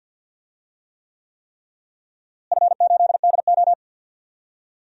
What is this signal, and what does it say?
Morse code, heard as a keyed tone, about 50 wpm, 700 Hz. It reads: F8DO
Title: DXCC-100 Call Signs - 4 Characters - Repeated 3x - 50wpm